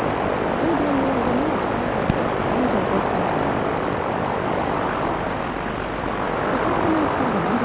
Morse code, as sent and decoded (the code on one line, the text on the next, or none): none